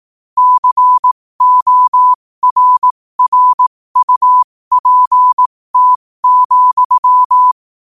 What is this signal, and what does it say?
Morse code, heard as a keyed tone, about 18 wpm, spaced as for 15 wpm, 1000 Hz. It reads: CORRUPT,